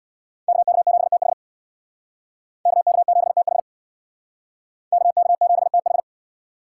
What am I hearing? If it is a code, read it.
Morse code, heard as a keyed tone, about 60 wpm, 700 Hz. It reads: KK7TH KK7TH KK7TH
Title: Speed Racing - USA 2x2 Call Signs 50wpm